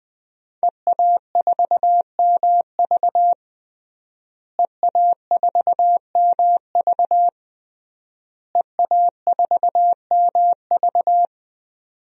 Morse code, heard as a keyed tone, about 20 wpm, 700 Hz. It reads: EA4MV EA4MV EA4MV
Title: DXCC-100 Call Signs - 5 Characters - Repeated 3x - 20wpm